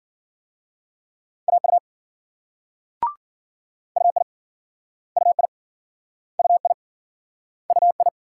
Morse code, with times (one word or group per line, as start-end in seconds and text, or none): 1.5-1.8 AU
4.0-4.2 UI
5.2-5.5 UI
6.4-6.7 UI
7.7-8.1 UI